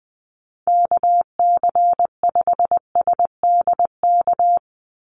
Morse code, heard as a keyed tone, about 20 wpm, 700 Hz. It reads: KC5SDK
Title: USA Call Signs 20wpm